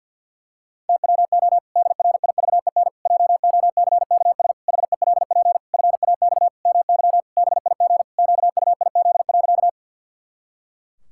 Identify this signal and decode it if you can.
Morse code, heard as a keyed tone, about 52 wpm, 700 Hz. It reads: TWO DRIVEN JOCKS HELP FAX MY BIG QUIZ.
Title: Pangram Challenge v2 - 50wpm